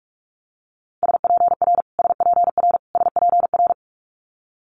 Morse code, heard as a keyed tone, about 46 wpm, 700 Hz. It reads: SPR SPR SPR
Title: Multi-letter Phonemes Repeated 3x 45wpm